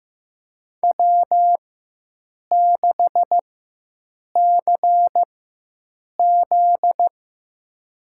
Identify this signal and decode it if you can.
Morse code, heard as a keyed tone, about 15 wpm, spaced as for 7 wpm, 700 Hz. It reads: W6CZ
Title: USA Call Signs - 4 Characters - 4x Character Spacing - 15wpm